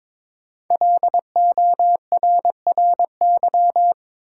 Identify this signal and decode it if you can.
Morse code, heard as a keyed tone, about 22 wpm, 700 Hz. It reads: LORRY
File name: 5 Letter Words - First 3 Letters Given - 22wpm